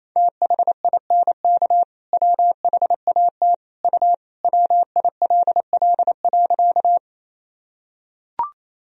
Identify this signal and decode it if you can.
Morse code, heard as a keyed tone, about 27 wpm, 700 Hz. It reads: THINK WHAT U WILL.
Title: Sentences from Top 200 Words and Abbreviations 28wpm